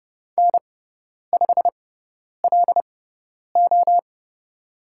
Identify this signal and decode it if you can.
Morse code, heard as a keyed tone, about 30 wpm, spaced as for 10 wpm, 700 Hz. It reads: N5LO
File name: Lesson 18g - Letter m - Call Signs 30wpm